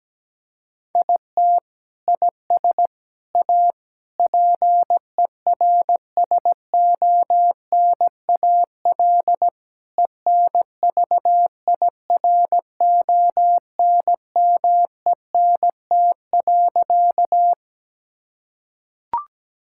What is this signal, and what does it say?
Morse code, heard as a keyed tone, about 17 wpm, 700 Hz. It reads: IT IS A PERSONAL ENVIRONMENT.
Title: Sentences from Top 1000 Words - Review Part 5 - 17wpm